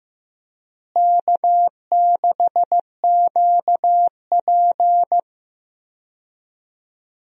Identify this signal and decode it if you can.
Morse code, heard as a keyed tone, about 15 wpm, 700 Hz. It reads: K6QP
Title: USA Call Signs - 4 Characters - Repeated 2x - 15wpm